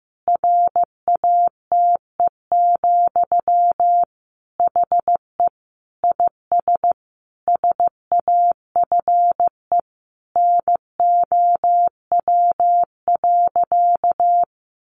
Morse code, heard as a keyed tone, about 15 wpm, 700 Hz. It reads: RATE, HE IS SAFE NOW.
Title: Sentences from Top 900 Words 15wpm